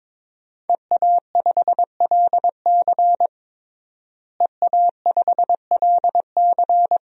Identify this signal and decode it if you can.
Morse code, heard as a keyed tone, about 22 wpm, 700 Hz. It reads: EA5LC EA5LC
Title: DXCC-100 Call Signs - 5 Characters - Repeated 2x - 22wpm